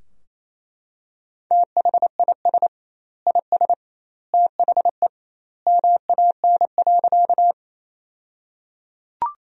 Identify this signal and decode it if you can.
Morse code, heard as a keyed tone, about 28 wpm, 700 Hz. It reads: THIS IS THE MAN.
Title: 2x Word-Spacing - Sentences from Top 200 Words - 28wpm